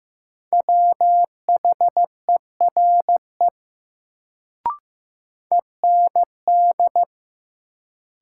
Morse code, heard as a keyed tone, about 15 wpm, 700 Hz. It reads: WHERE END
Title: Sentences from Top 200 Words - New Word Drill - 15wpm